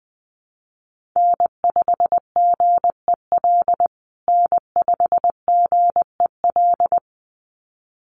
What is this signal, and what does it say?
Morse code, heard as a keyed tone, about 19 wpm, 700 Hz. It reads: N5GEL N5GEL
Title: USA Call Signs Repeated 2x 20wpm